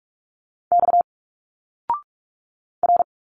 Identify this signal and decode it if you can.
Morse code, heard as a keyed tone, about 46 wpm, 700 Hz. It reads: X R
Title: Single Letters 45wpm